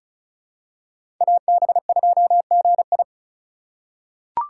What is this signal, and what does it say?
Morse code, heard as a keyed tone, about 36 wpm, 700 Hz. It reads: AB2GI
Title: K1USN SST QSO Practice - 35wpm